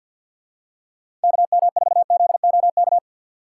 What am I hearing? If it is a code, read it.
Morse code, heard as a keyed tone, about 52 wpm, 700 Hz. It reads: KM4ZOK
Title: USA Call Signs - 6 Characters - 8x Character Spacing - 50wpm